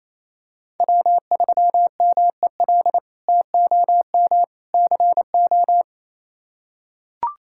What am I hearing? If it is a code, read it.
Morse code, heard as a keyed tone, about 28 wpm, 700 Hz. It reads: W3MEL TOM CO
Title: K1USN SST QSO Practice - 28wpm